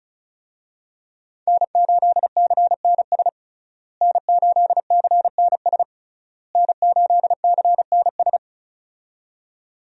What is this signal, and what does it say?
Morse code, heard as a keyed tone, about 34 wpm, 700 Hz. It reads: N8CNS N8CNS N8CNS
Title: USA Call Signs - 5 Characters - Repeated 3x - 35wpm